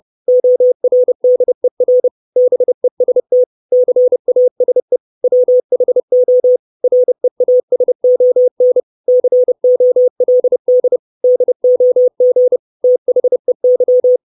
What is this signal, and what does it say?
Morse code, heard as a keyed tone, about 30 wpm, 500 Hz. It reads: ORDER BEST CASE WHO REASON COLD DOG THEY
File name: 632 Words 30WPM